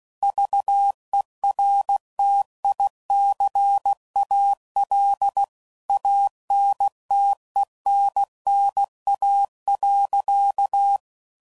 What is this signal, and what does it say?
Morse code, heard as a keyed tone, about 16 wpm, 800 Hz. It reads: VERTICAL ANTENNA.
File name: CW Code Groups @15WPM / QSOs / kc7sac